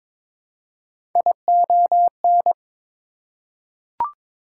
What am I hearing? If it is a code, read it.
Morse code, heard as a keyed tone, about 21 wpm, 700 Hz. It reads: ION
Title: Prefixes and Suffixes 22wpm